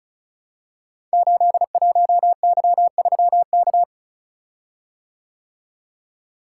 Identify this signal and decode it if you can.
Morse code, heard as a keyed tone, about 36 wpm, 700 Hz. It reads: 81Y3K
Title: Code Groups - 5 Letter:Numbers - 35wpm